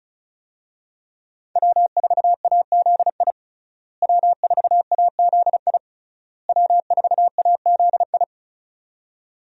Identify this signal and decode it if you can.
Morse code, heard as a keyed tone, about 36 wpm, 700 Hz. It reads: W4AZI W4AZI W4AZI
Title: USA Call Signs - 5 Characters - Repeated 3x - 35wpm